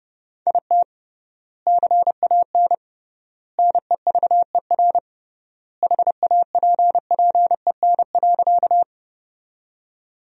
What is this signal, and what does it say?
Morse code, heard as a keyed tone, about 32 wpm, 700 Hz. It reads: IT CAN NEVER HAPPEN.
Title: 3x Word-Spacing - Sentences from Top 400 Words - 30wpm